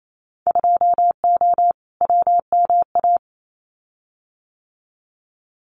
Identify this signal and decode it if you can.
Morse code, heard as a keyed tone, about 28 wpm, 690 Hz. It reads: 2O WMA
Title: Winter Field Day Exchanges Encore 28wpm